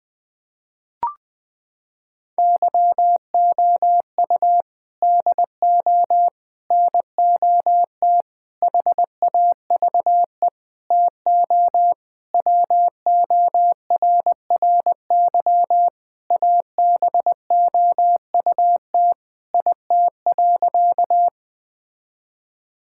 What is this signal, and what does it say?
Morse code, heard as a keyed tone, about 20 wpm, 700 Hz. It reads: YOU DO NOT HAVE TO WORRY ABOUT IT.
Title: Sentences with 5-Word Phrases - 20wpm